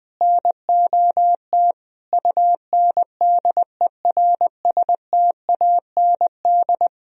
Morse code, heard as a keyed tone, about 20 wpm, 700 Hz. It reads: NOT UNDERSTAND
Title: Sentences from Top 1000 Words - Review Part 3 - 20wpm